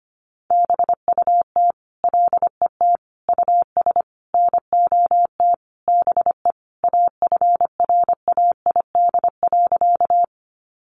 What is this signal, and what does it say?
Morse code, heard as a keyed tone, about 25 wpm, 700 Hz. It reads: BUT LET US NOT BE AFRAID.
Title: Sentences from Top 1500 Words 25wpm